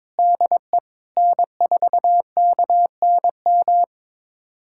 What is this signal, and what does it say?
Morse code, heard as a keyed tone, about 22 wpm, 700 Hz. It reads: DE N4KNM
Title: USA Callsign DE Callsign - 5 Characters - 22wpm